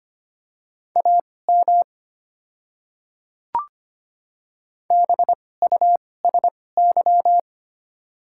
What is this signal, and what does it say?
Morse code, heard as a keyed tone, about 25 wpm, spaced as for 18 wpm, 700 Hz. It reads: AM BUSY
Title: 2x Character Spacing - Top 500 Words - 25wpm